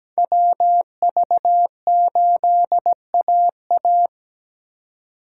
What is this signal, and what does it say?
Morse code, heard as a keyed tone, about 20 wpm, 700 Hz. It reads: WV8AA
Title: Speed Racing - USA 2x2 Call Signs 17wpm